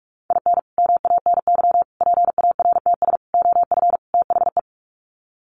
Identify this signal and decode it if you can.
Morse code, heard as a keyed tone, about 44 wpm, 700 Hz. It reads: IN MANY PARTS OF THE
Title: Sets of 5 Words Encore 45wpm